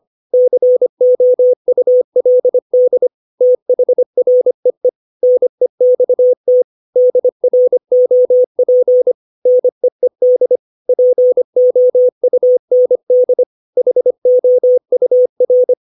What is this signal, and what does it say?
Morse code, heard as a keyed tone, about 25 wpm, 500 Hz. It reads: COULD THREE NEXT DROP NEED POUND HOUR
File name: 632 words / 632 Words 25WPM